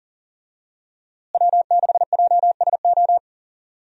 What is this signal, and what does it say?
Morse code, heard as a keyed tone, about 40 wpm, 700 Hz. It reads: W6JSO
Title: USA Call Signs - 5 Characters - Repeated 3x - 40wpm